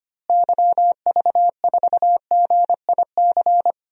Morse code, heard as a keyed tone, about 25 wpm, 700 Hz. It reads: YV4GIC
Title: DXCC-100 Call Signs - 6 Characters - Repeated 3x - 25wpm